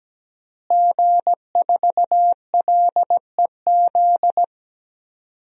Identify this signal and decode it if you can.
Morse code, heard as a keyed tone, about 17 wpm, 700 Hz. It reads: G4LEZ
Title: DXCC-100 Call Signs - 5 Characters - Repeated 2x - 17wpm